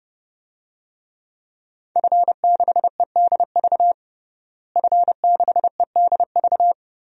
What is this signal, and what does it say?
Morse code, heard as a keyed tone, about 30 wpm, 700 Hz. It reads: F6EDV F6EDV
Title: DXCC-100 Call Signs - 5 Characters - Repeated 2x - 30wpm